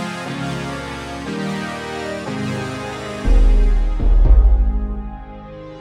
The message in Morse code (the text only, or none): none